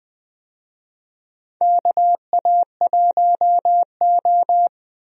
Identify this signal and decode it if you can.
Morse code, heard as a keyed tone, about 21 wpm, 700 Hz. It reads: KA1O
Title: USA Call Signs - 4 Characters - Repeated 3x - 20wpm